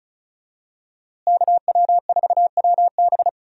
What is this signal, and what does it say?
Morse code, heard as a keyed tone, about 34 wpm, 700 Hz. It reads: KW4WB
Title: USA Call Signs - 5 Characters - Repeated 2x - 35wpm